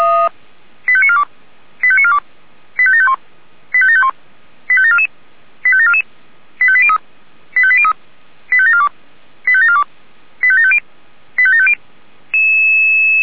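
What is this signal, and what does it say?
Five-tone selective call (ZVEI), heard as quick runs of five tones, tones of about 70 ms. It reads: 85942 85942 85751 85751 85744 85744 85803 85803 85742 85742 85750 85750
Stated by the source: ZVEI five-tone selective call received on an SDR.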